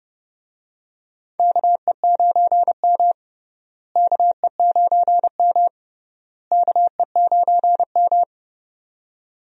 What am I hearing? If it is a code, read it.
Morse code, heard as a keyed tone, about 32 wpm, 700 Hz. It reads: KE9M KE9M KE9M
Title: USA Call Signs - 4 Characters - Repeated 3x - 30wpm